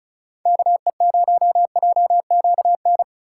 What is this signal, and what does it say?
Morse code, heard as a keyed tone, about 36 wpm, 690 Hz. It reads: KE0JQN
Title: USA Call Signs - 6 Characters - 3x Character Spacing - 35wpm